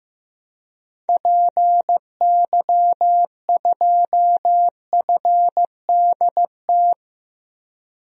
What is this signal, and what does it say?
Morse code, heard as a keyed tone, about 15 wpm, 700 Hz. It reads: PY2FDT
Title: DXCC-100 Call Signs - 6 Characters - Repeated 2x - 15wpm